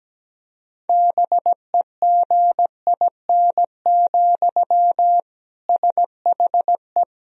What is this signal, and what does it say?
Morse code, heard as a keyed tone, about 17 wpm, 700 Hz. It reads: BEGIN, SHE
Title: Sentences from Top 1000 Words - Review Part 1 - 17wpm